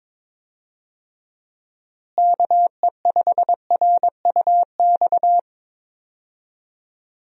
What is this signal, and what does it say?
Morse code, heard as a keyed tone, about 22 wpm, 700 Hz. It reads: KE5RUX
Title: USA Call Signs - 6 Characters - Repeated 2x - 22wpm